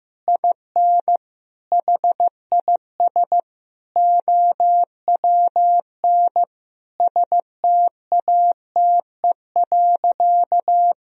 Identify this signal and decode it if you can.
Morse code, heard as a keyed tone, about 15 wpm, 700 Hz. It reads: IN HIS OWN STATE.
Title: Sentences from Top 1000 Words - Review Part 3 - 15wpm